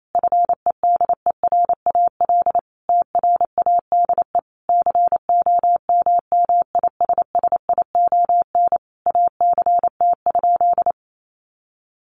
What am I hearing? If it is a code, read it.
Morse code, heard as a keyed tone, about 28 wpm, 700 Hz. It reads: FEDERAL TRADE COMMISSION ACT?
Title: Sentences from Top 1400 Words 28wpm